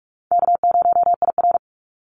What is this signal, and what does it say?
Morse code, heard as a keyed tone, about 44 wpm, 700 Hz. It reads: K0IR
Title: Call Signs - 45wpm